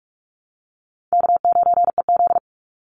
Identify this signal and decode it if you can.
Morse code, heard as a keyed tone, about 46 wpm, 690 Hz. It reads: K9EZ
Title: NA SOTA Chaser Callsigns 45wpm